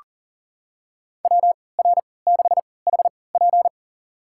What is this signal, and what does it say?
Morse code, heard as a keyed tone, about 40 wpm, spaced as for 23 wpm, 700 Hz. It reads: WR6HP
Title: USA Call Signs - 5 Characters - 3x Character Spacing - 40wpm